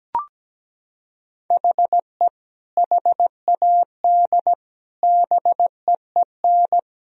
Morse code, heard as a keyed tone, about 17 wpm, 700 Hz. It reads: HE HAD BEEN
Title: Sentences from Top 1000 Words - Review Part 4 - 17wpm